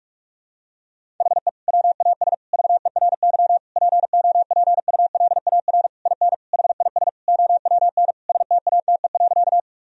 Morse code, heard as a keyed tone, about 46 wpm, 700 Hz. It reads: HE WAS VERY POPULAR IN HIS OWN STATE.
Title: Sentences from Top 1000 Words - Review Part 3 - 45wpm